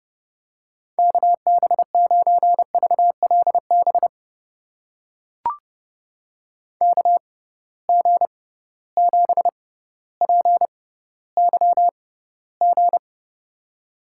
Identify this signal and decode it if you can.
Morse code, heard as a keyed tone, about 30 wpm, spaced as for 15 wpm, 700 Hz. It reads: KB9VLB KG7PYG
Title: USA Call Signs - 6 Characters - 6x Character Spacing - 30wpm